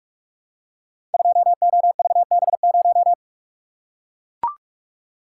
Morse code, heard as a keyed tone, about 44 wpm, 700 Hz. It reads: 2OVB0
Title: Code Groups - 5 Letter:Numbers - 45wpm